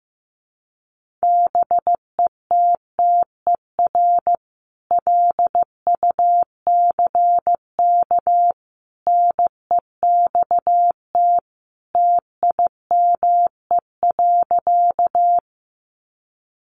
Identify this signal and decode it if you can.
Morse code, heard as a keyed tone, about 15 wpm, 700 Hz. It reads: BETTER LUCK NEXT TIME.
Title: Sentences from Top 1500 Words 15wpm